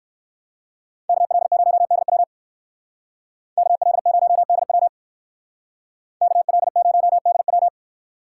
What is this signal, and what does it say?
Morse code, heard as a keyed tone, about 59 wpm, 700 Hz. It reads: KL0DW KL0DW KL0DW